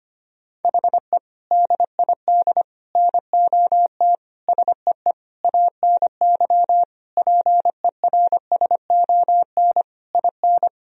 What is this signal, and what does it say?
Morse code, heard as a keyed tone, about 25 wpm, 700 Hz. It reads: HE DID NOT SEE ANY PERSON IN
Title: Sentences from Top 1000 Words - Review Part 3 - 25wpm